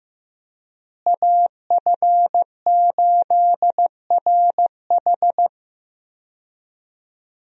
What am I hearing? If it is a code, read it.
Morse code, heard as a keyed tone, about 16 wpm, 700 Hz. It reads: AF8RH
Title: USA Call Signs - 5 Characters - Repeated 2x - 15wpm